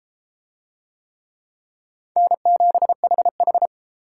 Morse code, heard as a keyed tone, about 31 wpm, 700 Hz. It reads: N7HH